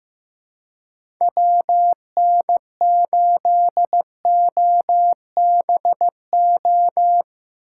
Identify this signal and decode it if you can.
Morse code, heard as a keyed tone, about 15 wpm, 700 Hz. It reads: WN8OBO
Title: USA Call Signs - 4x Character Spacing - 15wpm